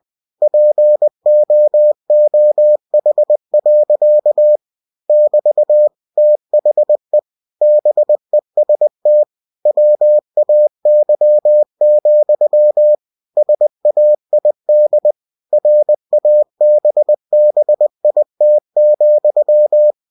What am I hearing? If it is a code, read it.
Morse code, heard as a keyed tone, about 20 wpm, 600 Hz. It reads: POOH. <BT> THE BEST WAY, SAID RABBIT,
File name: cw-books / winnie-the-pooh / winnie-20wpm-page-0061